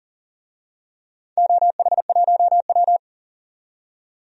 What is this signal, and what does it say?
Morse code, heard as a keyed tone, about 40 wpm, 700 Hz. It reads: OH1W
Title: DXCC-100 Call Signs - 4 Characters - 40wpm